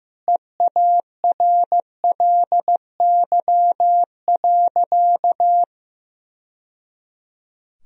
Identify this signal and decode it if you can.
Morse code, heard as a keyed tone, about 15 wpm, 700 Hz. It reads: EARLY.
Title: Sentences from Top 1000 Words - Review Part 4 - 15wpm